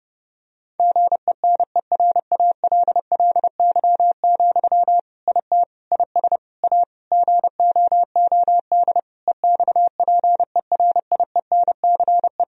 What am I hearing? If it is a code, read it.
Morse code, heard as a keyed tone, about 30 wpm, 700 Hz. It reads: GENERALLY, IT IS A GOOD EXPERIENCE
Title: Sentences from Top 1000 Words - Review Part 4 - 30wpm